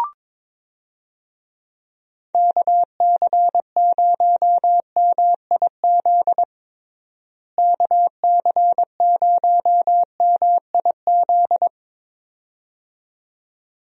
Morse code, heard as a keyed tone, about 22 wpm, 700 Hz. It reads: KC0MIZ KC0MIZ
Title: USA Call Signs - 6 Characters - Repeated 2x - 22wpm